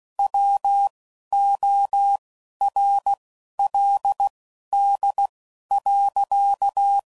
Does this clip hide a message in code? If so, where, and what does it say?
Yes: Morse code from 0.2 to 7.0 s, heard as a keyed tone, about 16 wpm, spaced as for 11 wpm, 800 Hz. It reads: WORLD.